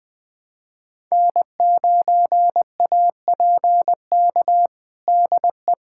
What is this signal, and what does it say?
Morse code, heard as a keyed tone, about 21 wpm, 700 Hz. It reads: N9APK DE
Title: USA Callsign DE Callsign - 5 Characters - 20wpm